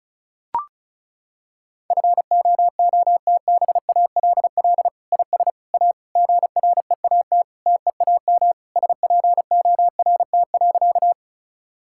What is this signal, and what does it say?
Morse code, heard as a keyed tone, about 36 wpm, 700 Hz. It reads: FOOTBALL IS A GREAT TEAM SPORT.
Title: Sentences from Top 1600 Words 35wpm